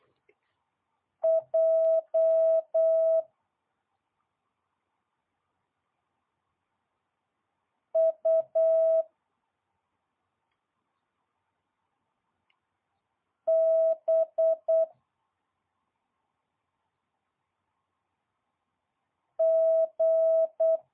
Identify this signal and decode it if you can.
Morse code, heard as a keyed tone, about 8 wpm, 650 Hz. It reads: J U B G
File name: jh8cbh / 2022cw